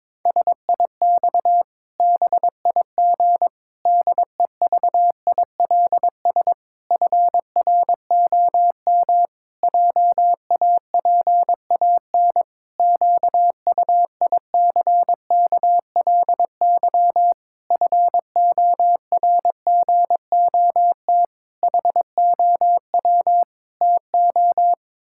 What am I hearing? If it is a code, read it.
Morse code, heard as a keyed tone, about 22 wpm, 700 Hz. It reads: SIX BIG DEVILS FROM JAPAN QUICKLY FORGOT HOW TO
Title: Pangram Challenge v2 - 22wpm